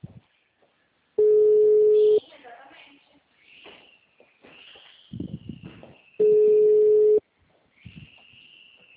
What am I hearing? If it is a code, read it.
Ringback tone (Europe, 425 Hz), with speech.